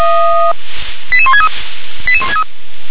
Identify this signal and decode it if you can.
Five-tone selective call (ZVEI), heard as quick runs of five tones, tones of about 70 ms. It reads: 88163 88163